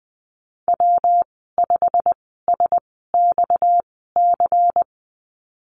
Morse code, heard as a keyed tone, about 20 wpm, spaced as for 15 wpm, 700 Hz. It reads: W5SXC
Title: 231 Lesson 37g - Letter x - Call Signs 20wpm